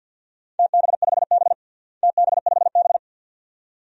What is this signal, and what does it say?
Morse code, heard as a keyed tone, about 49 wpm, 700 Hz. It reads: TB5B TB5B
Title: DXCC-100 Call Signs - 4 Characters - Repeated 2x - 50wpm